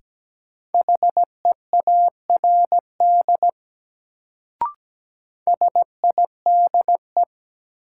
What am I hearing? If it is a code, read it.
Morse code, heard as a keyed tone, about 17 wpm, 700 Hz. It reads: HEARD SIDE
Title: Sentences from Top 400 Words - New Word Drill - 17wpm